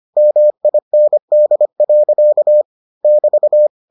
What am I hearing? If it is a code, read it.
Morse code, heard as a keyed tone, about 25 wpm, 600 Hz. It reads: MIND. <BT>